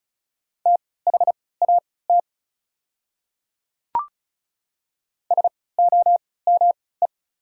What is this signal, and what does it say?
Morse code, heard as a keyed tone, about 34 wpm, spaced as for 20 wpm, 700 Hz. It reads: THAT SOME
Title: 3x Character Spacing - Top 200 Words - 35wpm